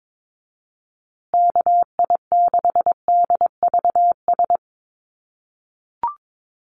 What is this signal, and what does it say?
Morse code, heard as a keyed tone, about 22 wpm, 700 Hz. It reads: KI6DVS
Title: USA Call Signs Repeated 2x 22wpm